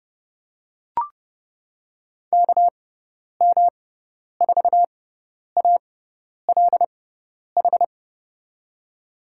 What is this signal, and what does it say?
Morse code, heard as a keyed tone, about 30 wpm, spaced as for 10 wpm, 700 Hz. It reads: KM4ALH
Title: USA Call Signs - 6 Characters - 6x Character Spacing - 30wpm